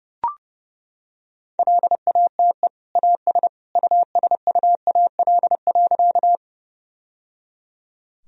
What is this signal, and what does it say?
Morse code, heard as a keyed tone, about 30 wpm, 700 Hz. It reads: LATE AS USUAL.
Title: Sentences from Top 1000 Words - Review Part 2 - 30wpm